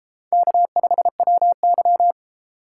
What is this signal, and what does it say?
Morse code, heard as a keyed tone, about 32 wpm, 700 Hz. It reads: K5WY